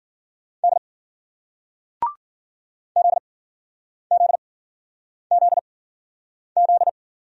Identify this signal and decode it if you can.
Morse code, heard as a keyed tone, about 52 wpm, spaced as for 9 wpm, 700 Hz. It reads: D ZZZZ